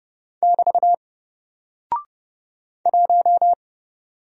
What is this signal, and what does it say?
Morse code, heard as a keyed tone, about 30 wpm, 700 Hz. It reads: <BT> 1
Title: Single Character (Letter:Number:Punctuation:Prosign) Alt 30wpm